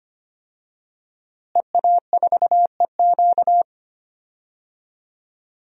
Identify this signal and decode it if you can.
Morse code, heard as a keyed tone, about 25 wpm, 700 Hz. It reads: EA4EQ
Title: DXCC-100 Call Signs - 5 Characters - Repeated 3x - 25wpm